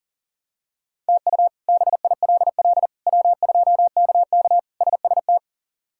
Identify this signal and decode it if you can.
Morse code, heard as a keyed tone, about 40 wpm, 700 Hz. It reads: TU BILL W2KK SST
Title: K1USN SST QSO Practice - 40wpm